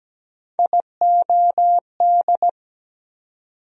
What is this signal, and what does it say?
Morse code, heard as a keyed tone, about 17 wpm, 700 Hz. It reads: IOD
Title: USA Call Sign Postfixes - 3 Letters 17wpm